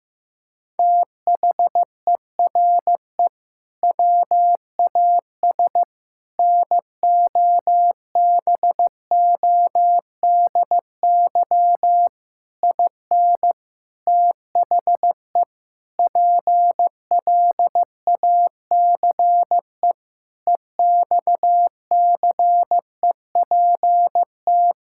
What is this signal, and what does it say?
Morse code, heard as a keyed tone, about 15 wpm, 700 Hz. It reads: THERE WAS NOBODY IN THE PLACE EXCEPT
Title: Sentences from Top 1600 Words 15wpm